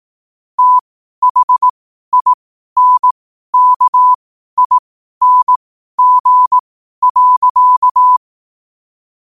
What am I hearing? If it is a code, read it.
Morse code, heard as a keyed tone, about 18 wpm, spaced as for 12 wpm, 1000 Hz. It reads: THINKING.
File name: qotdfiles / 12wpm